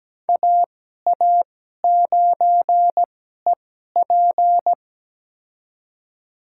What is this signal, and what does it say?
Morse code, heard as a keyed tone, about 17 wpm, spaced as for 12 wpm, 700 Hz. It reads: AA9EP
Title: USA Call Signs - 5 Characters - 2x Character Spacing - 17wpm